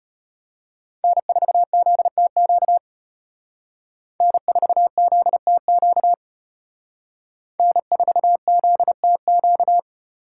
Morse code, heard as a keyed tone, about 32 wpm, 700 Hz. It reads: N4ZTQ N4ZTQ N4ZTQ